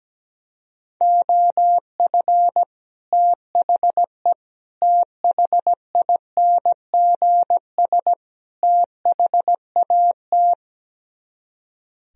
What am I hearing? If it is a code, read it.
Morse code, heard as a keyed tone, about 17 wpm, 700 Hz. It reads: OF THE THINGS THAT
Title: Speed-Racing - Sets of 4 Words 17wpm